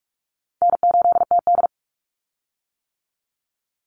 Morse code, heard as a keyed tone, about 44 wpm, 700 Hz. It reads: N8TD
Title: Call Signs - 45wpm